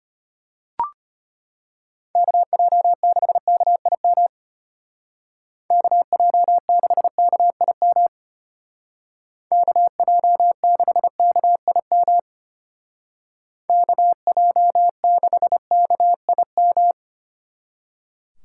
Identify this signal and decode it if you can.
Morse code, heard as a keyed tone, about 30 wpm, 700 Hz. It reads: KJ6KIM KJ6KIM KJ6KIM KJ6KIM
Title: Speed Racing - USA 2x3 Call Signs 25wpm